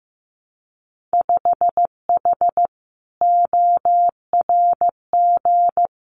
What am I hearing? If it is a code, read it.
Morse code, heard as a keyed tone, about 15 wpm, 700 Hz. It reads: 5H ORG
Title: Winter Field Day Exchanges Encore 15wpm